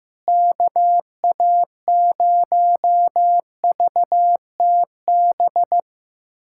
Morse code, heard as a keyed tone, about 15 wpm, 700 Hz. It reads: KA0VTB
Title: USA Callsign DE Callsign 15wpm